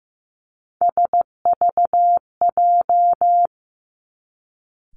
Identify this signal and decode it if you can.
Morse code, heard as a keyed tone, about 15 wpm, 700 Hz. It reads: SVJ
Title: Three Challenging Letter-Number Combinations 15wpm